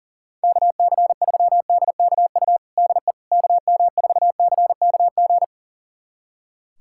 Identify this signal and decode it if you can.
Morse code, heard as a keyed tone, about 40 wpm, 700 Hz. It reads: KC3DKU DE KM4CKG
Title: USA Callsign DE Callsign 40wpm